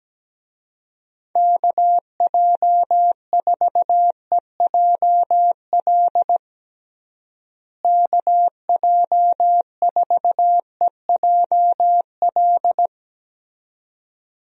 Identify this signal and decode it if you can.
Morse code, heard as a keyed tone, about 17 wpm, 700 Hz. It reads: KJ4EJL KJ4EJL
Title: USA Call Signs - 6 Characters - Repeated 2x - 17wpm